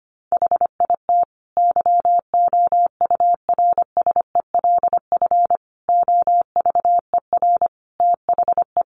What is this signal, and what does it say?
Morse code, heard as a keyed tone, about 25 wpm, 700 Hz. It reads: HIT YOURSELF OVER THE
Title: Sentences from Top 1200 Words 25wpm